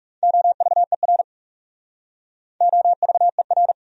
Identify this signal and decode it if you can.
Morse code, heard as a keyed tone, about 36 wpm, 700 Hz. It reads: OVER OVER
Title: Speed-Racing Top 200 Words 30wpm